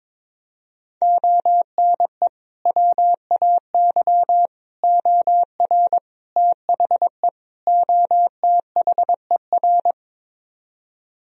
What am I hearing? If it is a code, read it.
Morse code, heard as a keyed tone, about 22 wpm, 700 Hz. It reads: ONE WAY OR THE OTHER
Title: Speed-Racing - Sets of 5 Words 22wpm